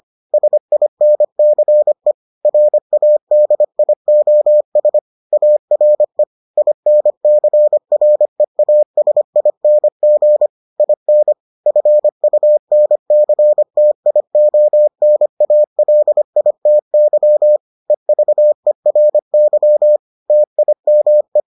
Morse code, heard as a keyed tone, about 25 wpm, 600 Hz. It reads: SINCE RADIOS ARE INCREASING IN FUNCTIONALITY EVERY TIME